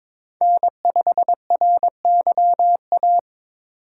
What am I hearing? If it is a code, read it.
Morse code, heard as a keyed tone, about 22 wpm, 700 Hz. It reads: N5RYA
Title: USA Call Signs - 5 Characters - 4x Character Spacing - 22wpm